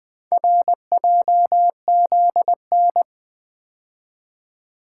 Morse code, heard as a keyed tone, about 24 wpm, 700 Hz. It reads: RJZN
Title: Speed-Racing - Four Letter-Number Combinations - 20wpm